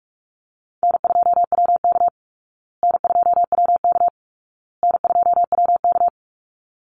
Morse code, heard as a keyed tone, about 44 wpm, 700 Hz. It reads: N2WK N2WK N2WK
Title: Call Signs Repeated 2x - 45wpm